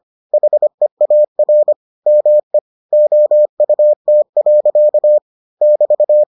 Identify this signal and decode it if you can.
Morse code, heard as a keyed tone, about 25 wpm, 600 Hz. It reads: HEAR ME OUT. <BT>